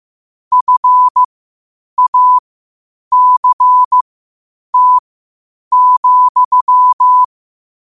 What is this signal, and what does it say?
Morse code, heard as a keyed tone, about 15 wpm, spaced as for 8 wpm, 1000 Hz. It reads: FACT,
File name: qotdfiles / 8wpm